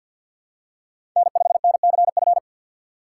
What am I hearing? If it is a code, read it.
Morse code, heard as a keyed tone, about 52 wpm, 700 Hz. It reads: N5NCF
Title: USA Call Signs - 5 Characters - 4x Character Spacing - 50wpm